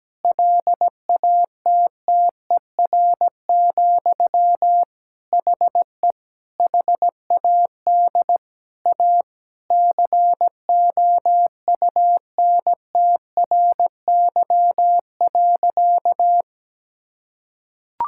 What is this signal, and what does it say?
Morse code, heard as a keyed tone, about 17 wpm, 700 Hz. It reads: LATTER, HE HAD A COUNTRY.
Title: Sentences from Top 1600 Words 17wpm